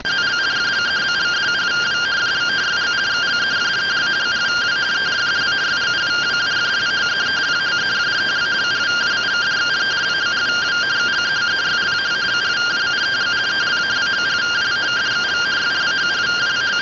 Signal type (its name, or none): two-tone teleprinter signal